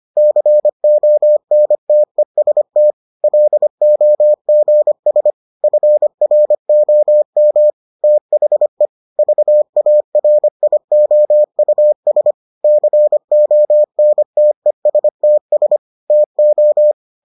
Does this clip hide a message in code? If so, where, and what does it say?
Yes: Morse code from 0.2 to 16.9 s, heard as a keyed tone, about 25 wpm, 600 Hz. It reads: CONTEST LOGS FROM THE VARIOUS CONTESTS TO